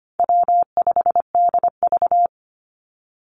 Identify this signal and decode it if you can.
Morse code, heard as a keyed tone, about 25 wpm, 700 Hz. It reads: W5DV
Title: Call Signs Repeated 3x - 25wpm